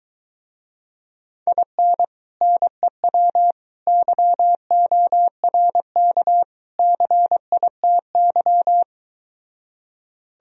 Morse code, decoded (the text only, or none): IN NEW YORK CITY